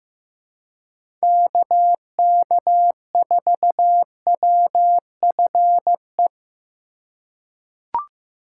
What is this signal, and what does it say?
Morse code, heard as a keyed tone, about 15 wpm, 700 Hz. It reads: KK4WFE
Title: USA Call Signs - 6 Characters - 2x Character Spacing - 15wpm